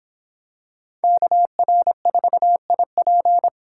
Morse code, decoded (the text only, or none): KR4IP